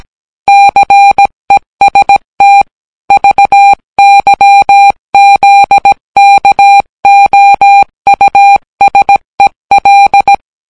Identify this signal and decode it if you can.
Morse code, heard as a keyed tone, about 17 wpm, 800 Hz. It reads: CEST VYZKOUSEL